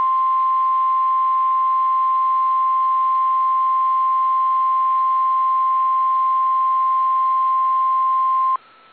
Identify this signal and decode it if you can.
Emergency Alert System (SAME) tones (the 1050 Hz attention tone). It is heard from 0.6 to 8.3 s.
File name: eas / KSDZKDJL / KSDZKDJL